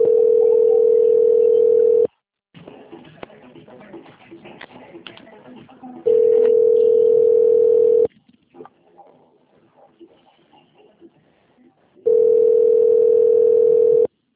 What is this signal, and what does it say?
Ringback tone (North America), 440 + 480 Hz.